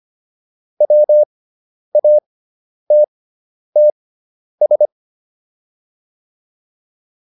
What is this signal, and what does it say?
Morse code, heard as a keyed tone, about 25 wpm, spaced as for 10 wpm, 600 Hz. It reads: WATTS
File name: QSO409 10